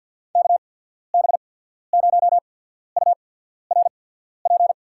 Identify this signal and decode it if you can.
Morse code, heard as a keyed tone, about 48 wpm, spaced as for 14 wpm, 700 Hz. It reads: KB0URP
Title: USA Call Signs - 6 Characters - 8x Character Spacing - 50wpm